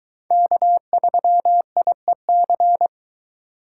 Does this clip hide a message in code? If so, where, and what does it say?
Yes: Morse code from 0.3 to 2.9 s, heard as a keyed tone, about 24 wpm, 700 Hz. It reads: K3IEC